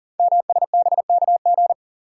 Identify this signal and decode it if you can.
Morse code, heard as a keyed tone, about 42 wpm, 690 Hz. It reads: MSBKG
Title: Code Groups - 5 Letters - 40wpm